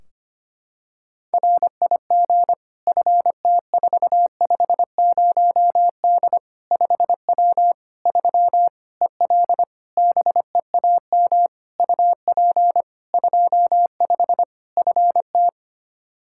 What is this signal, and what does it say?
Morse code, heard as a keyed tone, about 25 wpm, 700 Hz. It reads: RIG FT450D 5W 3 EL BEAM UP 25 FT
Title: Practice QSO - Exchange 2 - Info - 25wpm